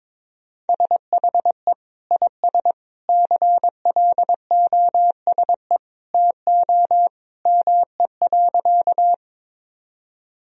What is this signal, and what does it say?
Morse code, heard as a keyed tone, about 22 wpm, 700 Hz. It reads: SHE IS CLOSE TO ME.
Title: Sentences from Top 1000 Words - Review Part 1 - 22wpm